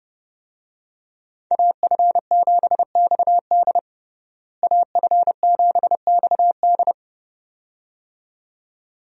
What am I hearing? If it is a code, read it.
Morse code, heard as a keyed tone, about 30 wpm, 700 Hz. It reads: AF7XD AF7XD
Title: USA Call Signs - 5 Characters - Repeated 2x - 30wpm